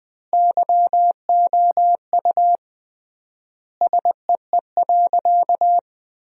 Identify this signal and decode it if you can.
Morse code, heard as a keyed tone, about 20 wpm, 700 Hz. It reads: YOU SEE.
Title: 3x Word-Spacing - Sentences from Top 100 Words - 20wpm